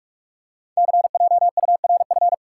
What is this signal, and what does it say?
Morse code, heard as a keyed tone, about 44 wpm, 700 Hz. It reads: CJURF
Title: Code Groups - 5 Letters - 45wpm